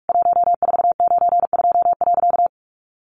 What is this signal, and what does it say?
Morse code, heard as a keyed tone, about 44 wpm, 700 Hz. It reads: 1492.
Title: A Sound Of Thunder 45wpm